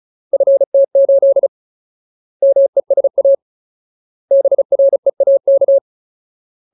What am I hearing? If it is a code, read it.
Morse code, heard as a keyed tone, about 34 wpm, 550 Hz. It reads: FT8 MESA BREAK
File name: Ultimate Word List / 35WPM Ultimate